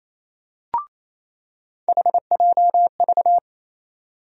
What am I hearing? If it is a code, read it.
Morse code, heard as a keyed tone, about 28 wpm, 700 Hz. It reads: HJV
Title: USA Call Sign Postfixes - 3 Letters 28wpm